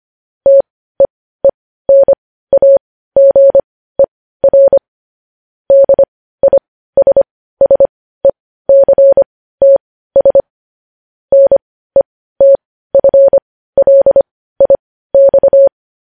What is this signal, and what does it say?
Morse code, heard as a keyed tone, about 25 wpm, spaced as for 15 wpm, 550 Hz. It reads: TEENAGER DISSECTS NETFLIX
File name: podcast / News headlines-1x-25x15